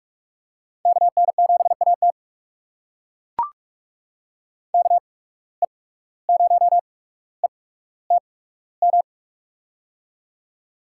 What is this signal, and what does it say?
Morse code, heard as a keyed tone, about 44 wpm, spaced as for 18 wpm, 700 Hz. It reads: KN7AT KE0ETM
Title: USA Call Signs - 8x Character Spacing - 45wpm